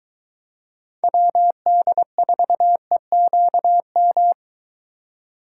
Morse code, heard as a keyed tone, about 23 wpm, 700 Hz. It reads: WD4EQM